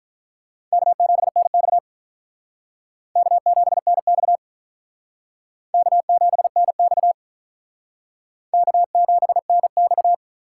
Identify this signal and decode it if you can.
Morse code, heard as a keyed tone, about 45 wpm, 700 Hz. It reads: K7NX K7NX K7NX K7NX